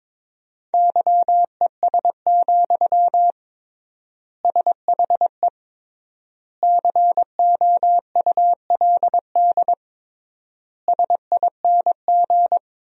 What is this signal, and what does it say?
Morse code, heard as a keyed tone, about 21 wpm, 700 Hz. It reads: YES, SHE COULD SING
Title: 3x Word-Spacing - Sentences from Top 500 Words - 22wpm